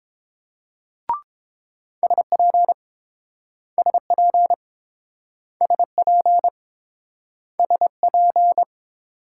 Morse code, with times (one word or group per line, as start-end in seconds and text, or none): 2.0-2.7 SP
3.8-4.5 SP
5.6-6.5 SP
7.6-8.6 SP